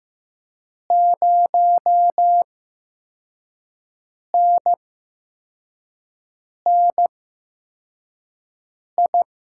Morse code, heard as a keyed tone, about 15 wpm, spaced as for 4 wpm, 700 Hz. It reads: 0NNI
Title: USA Call Signs - 6 Characters - 8x Character Spacing - 15wpm